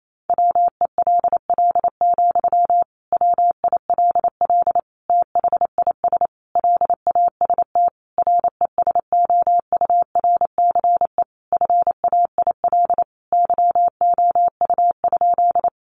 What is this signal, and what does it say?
Morse code, heard as a keyed tone, about 28 wpm, 700 Hz. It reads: WELL, WILL THIS LAST RESOURCE FAIL YOU?
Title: Sentences from Top 1300 Words 28wpm